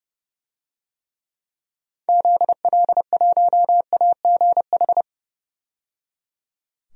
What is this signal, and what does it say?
Morse code, heard as a keyed tone, about 30 wpm, 700 Hz. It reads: ZL1AGH
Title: DXCC-100 Call Signs - 6 Characters - 30wpm